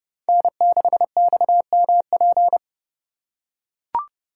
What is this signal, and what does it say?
Morse code, heard as a keyed tone, about 32 wpm, 700 Hz. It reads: N6XMP
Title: USA Call Signs - 4x Character Spacing - 30wpm